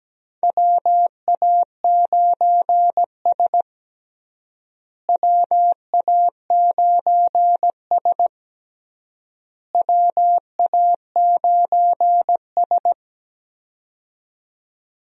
Morse code, heard as a keyed tone, about 17 wpm, 700 Hz. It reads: WA9S WA9S WA9S
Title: USA Call Signs - 4 Characters - Repeated 3x - 17wpm